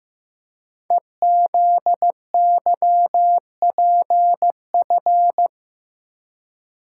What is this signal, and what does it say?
Morse code, heard as a keyed tone, about 15 wpm, 700 Hz. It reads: EZYPF
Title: Code Groups - 5 Letters - 15wpm